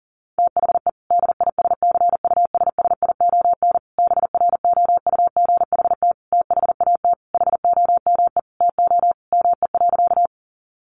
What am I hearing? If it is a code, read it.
Morse code, heard as a keyed tone, about 40 wpm, 700 Hz. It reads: THE DISCUSSION BROUGHT THAT HOME TO ME.
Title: Sentences from Top 1100 Words 40wpm